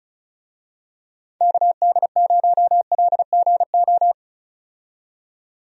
Morse code, heard as a keyed tone, about 36 wpm, 700 Hz. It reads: KD0LGO
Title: USA Call Signs - 6 Characters - 35wpm